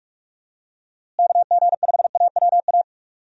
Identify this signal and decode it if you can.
Morse code, heard as a keyed tone, about 46 wpm, 700 Hz. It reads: KG5AWA
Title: USA Call Signs - 6 Characters - Repeated 2x - 45wpm